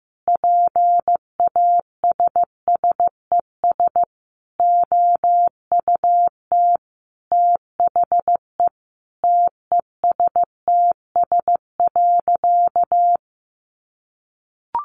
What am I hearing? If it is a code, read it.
Morse code, heard as a keyed tone, about 15 wpm, 700 Hz. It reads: PASSES OUT THE TESTS.
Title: Additional Easy Phrases 15wpm